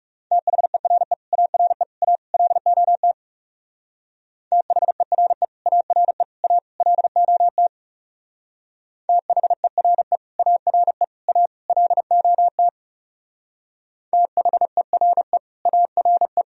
Morse code, heard as a keyed tone, about 35 wpm, 700 Hz. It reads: THERE ARE A LOT THERE ARE A LOT THERE ARE A LOT THERE ARE
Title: Speed-Racing - Sets of 4 Words 30wpm